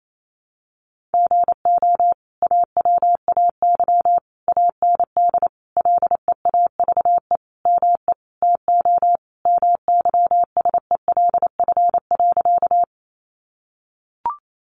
Morse code, heard as a keyed tone, about 28 wpm, 700 Hz. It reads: GO AWAY AND LEAVE ME TO MYSELF.
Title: Sentences from Top 1300 Words 28wpm